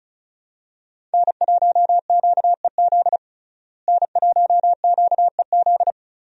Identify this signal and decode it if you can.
Morse code, heard as a keyed tone, about 36 wpm, 700 Hz. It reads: N1QEZ N1QEZ
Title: USA Call Signs - 5 Characters - Repeated 3x - 35wpm